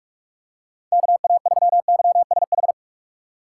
Morse code, heard as a keyed tone, about 46 wpm, 700 Hz. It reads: KA3YSH
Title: USA Call Signs - 6 Characters - 4x Character Spacing - 45wpm